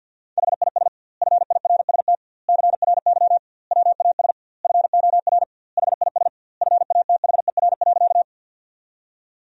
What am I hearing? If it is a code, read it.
Morse code, heard as a keyed tone, about 51 wpm, 700 Hz. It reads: HIS FIRST CRY WAS FOR HIS FATHER.
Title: 2x Word-Spacing - Sentences from Top 500 Words - 50wpm